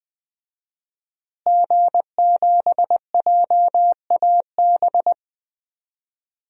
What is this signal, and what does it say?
Morse code, heard as a keyed tone, about 21 wpm, 700 Hz. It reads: G7JAB
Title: DXCC-100 Call Signs - 5 Characters - Repeated 2x - 20wpm